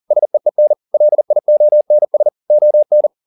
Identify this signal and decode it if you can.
Morse code, heard as a keyed tone, about 40 wpm, 600 Hz. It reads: SEEN LIONS ON